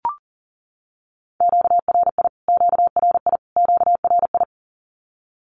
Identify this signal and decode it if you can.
Morse code, heard as a keyed tone, about 40 wpm, 700 Hz. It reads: QRI QRI QRI
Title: Q-Codes Repeated 3x 40wpm